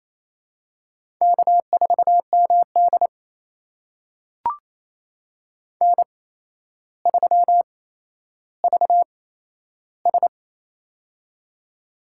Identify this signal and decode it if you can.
Morse code, heard as a keyed tone, about 28 wpm, spaced as for 12 wpm, 700 Hz. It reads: K4MD N3VS
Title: USA Call Signs - 4 Characters - 8x Character Spacing - 28wpm